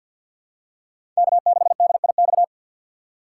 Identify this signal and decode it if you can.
Morse code, heard as a keyed tone, about 51 wpm, 700 Hz. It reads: K6DIX
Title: USA Call Signs - 5 Characters - 3x Character Spacing - 50wpm